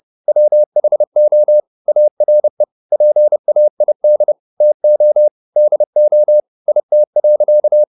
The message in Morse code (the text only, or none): WHO ARE PAID TO DO IT.